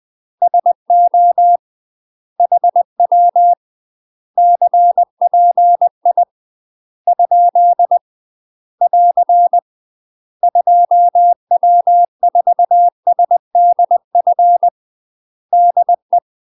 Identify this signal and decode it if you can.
Morse code, heard as a keyed tone, about 20 wpm, 700 Hz. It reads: SO HW CPI ? <AR> 2W4SDF DE
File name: UK MASTER QSOs TRAINING / 20WPM / UK MASTER QSOs 0069